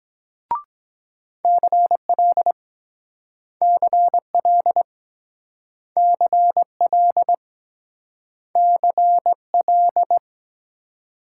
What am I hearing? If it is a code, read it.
Morse code, heard as a keyed tone, about 23 wpm, 700 Hz. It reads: CL CL CL CL